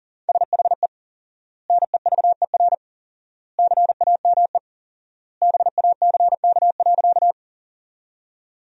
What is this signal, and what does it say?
Morse code, heard as a keyed tone, about 40 wpm, 700 Hz. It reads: SHE NEVER CAME BACK.
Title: 4x Word-Spacing - Sentences from Top 300 Words - 40wpm